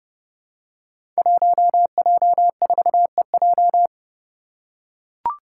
Morse code, heard as a keyed tone, about 31 wpm, 700 Hz. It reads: 1J4EJ
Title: Five Challenging Letter-Number Combinations 30wpm